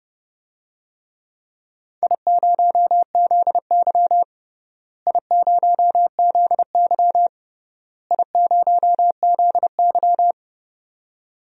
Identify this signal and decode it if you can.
Morse code, heard as a keyed tone, about 30 wpm, 700 Hz. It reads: I0ZY I0ZY I0ZY
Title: DXCC-100 Call Signs - 4 Characters - Repeated 3x - 30wpm